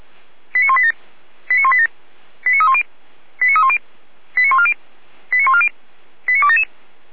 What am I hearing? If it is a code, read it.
Five-tone selective call (ZVEI), heard as quick runs of five tones, tones of about 70 ms. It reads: 89187 89187 89310 89310 89140 89140 89277
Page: /455/SDR/ZVEI/